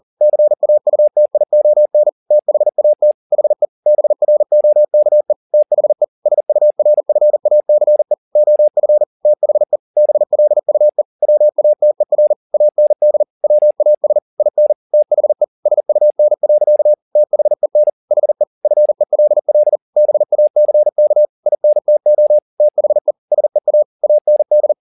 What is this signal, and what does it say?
Morse code, heard as a keyed tone, about 40 wpm, 600 Hz. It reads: CAUTION THAT HE BROKE THE SURFACE OF THE BLUE WATER AND WAS IN THE SUN. THEN HE FELL BACK INTO THE SEA AND
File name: the old man and the sea 2 40wpm